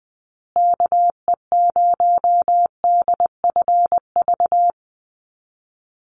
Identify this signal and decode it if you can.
Morse code, heard as a keyed tone, about 20 wpm, 700 Hz. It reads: KE0DFV
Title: USA Call Signs Repeated 3x 20wpm